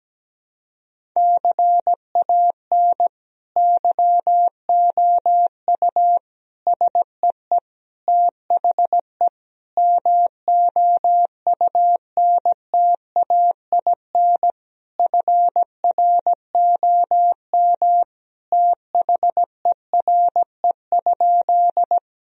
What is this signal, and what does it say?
Morse code, heard as a keyed tone, about 17 wpm, 700 Hz. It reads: CAN YOU SEE THE MOUNTAIN FROM THERE?
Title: Sentences from Top 1000 Words - Review Part 1 - 17wpm